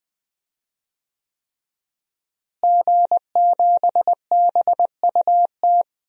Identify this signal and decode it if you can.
Morse code, heard as a keyed tone, about 20 wpm, 700 Hz. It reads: G7BUT
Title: DXCC-100 Call Signs - 5 Characters - Repeated 3x - 20wpm